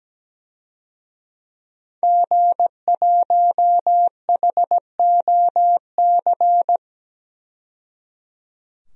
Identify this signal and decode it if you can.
Morse code, heard as a keyed tone, about 17 wpm, 700 Hz. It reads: G1HOC
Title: DXCC-100 Call Signs - 5 Characters - 17wpm